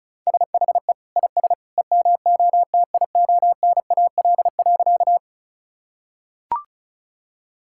Morse code, heard as a keyed tone, about 36 wpm, 690 Hz. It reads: SHE IS EMOTIONAL.
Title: Sentences from Top 1000 Words - Review Part 5 - 35wpm